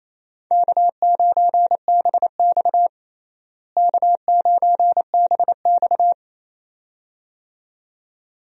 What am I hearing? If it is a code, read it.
Morse code, heard as a keyed tone, about 27 wpm, 700 Hz. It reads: K9BX K9BX
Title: USA Call Signs - 4 Characters - Repeated 2x - 28wpm